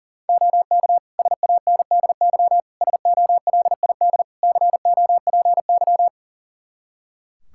Morse code, heard as a keyed tone, about 40 wpm, 700 Hz. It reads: OK SANDY SOLID COPY
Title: K1USN SST QSO Practice 40wpm